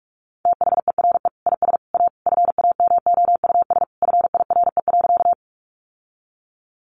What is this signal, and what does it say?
Morse code, heard as a keyed tone, about 44 wpm, 700 Hz. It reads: THERE IS A FAMOUS FIRE.
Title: Sentences from Top 1500 Words 45wpm